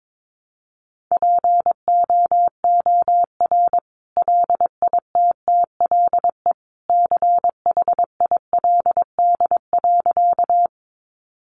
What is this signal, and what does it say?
Morse code, heard as a keyed tone, about 22 wpm, 700 Hz. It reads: POOR LITTLE CHILD.
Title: Sentences from Top 1000 Words 22wpm